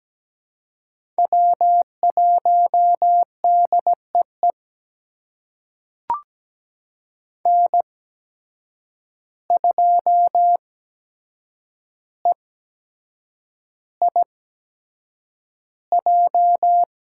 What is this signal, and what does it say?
Morse code, heard as a keyed tone, about 17 wpm, spaced as for 7 wpm, 700 Hz. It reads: W1DEE N2EIJ
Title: USA Call Signs - 5 Characters - 8x Character Spacing - 17wpm